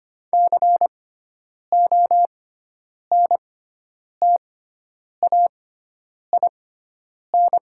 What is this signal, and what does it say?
Morse code, heard as a keyed tone, about 25 wpm, spaced as for 9 wpm, 700 Hz. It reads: CONTAIN
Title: 6x Character Spacing - Top 500 Words - 25wpm